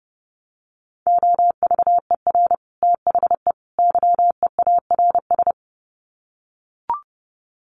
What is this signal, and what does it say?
Morse code, heard as a keyed tone, about 30 wpm, 700 Hz. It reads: OVER THE YEARS
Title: Sets of 3 Words 30wpm